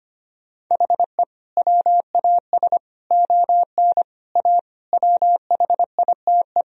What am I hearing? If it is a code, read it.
Morse code, heard as a keyed tone, about 25 wpm, 700 Hz. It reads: HE WAS ON A WHITE
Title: Sentences from Top 1000 Words - Review Part 1 - 25wpm